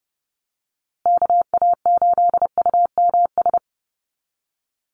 Morse code, heard as a keyed tone, about 30 wpm, 700 Hz. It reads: KA8UMS
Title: USA Call Signs Repeated 2x 30wpm